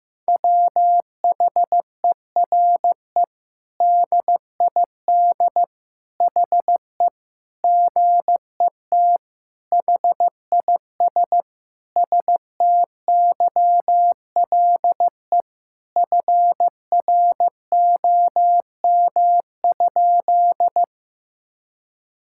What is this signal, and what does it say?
Morse code, heard as a keyed tone, about 15 wpm, 700 Hz. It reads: WHERE DID HE GET HIS STYLE FROM?
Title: Sentences from Top 1000 Words - Review Part 5 - 15wpm